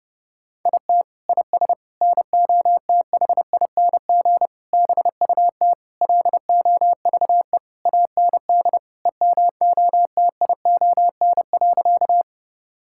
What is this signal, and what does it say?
Morse code, heard as a keyed tone, about 30 wpm, 700 Hz. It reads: IT IS NOTHING BUT LOVE AND EMOTION.
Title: Sentences from Top 1600 Words 30wpm